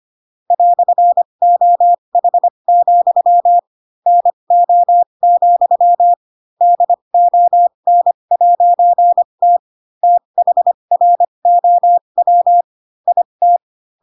Morse code, heard as a keyed tone, about 25 wpm, spaced as for 22 wpm, 700 Hz. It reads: "OH, NO, DON'T THROW IT
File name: Pinocchio / Pinocchio ch7 18 25 ik1hge